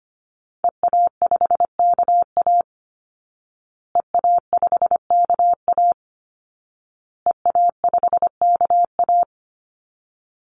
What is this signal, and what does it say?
Morse code, heard as a keyed tone, about 25 wpm, 700 Hz. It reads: EA5KA EA5KA EA5KA
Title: Call Signs Repeated 2x - 25wpm